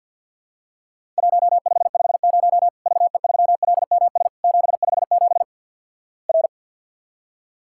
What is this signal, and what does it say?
Morse code, heard as a keyed tone, about 50 wpm, 700 Hz. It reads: 1550 VE3LMS 757 R
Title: Worked All Europe DX Contest - QTC Exchange with Repeats - 50wpm